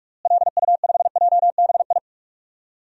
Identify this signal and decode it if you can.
Morse code, heard as a keyed tone, about 46 wpm, 700 Hz. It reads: LU5JBI
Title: DXCC-100 Call Signs - 6 Characters - Repeated 3x - 45wpm